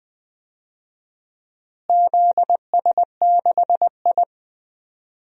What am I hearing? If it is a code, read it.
Morse code, heard as a keyed tone, about 20 wpm, 700 Hz. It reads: ZS6I
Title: DXCC-100 Call Signs - 4 Characters - Repeated 2x - 20wpm